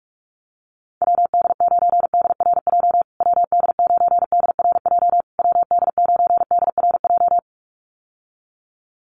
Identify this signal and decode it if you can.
Morse code, heard as a keyed tone, about 46 wpm, 700 Hz. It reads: WD9DRJ WD9DRJ WD9DRJ
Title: USA Call Signs Repeated 3x 45wpm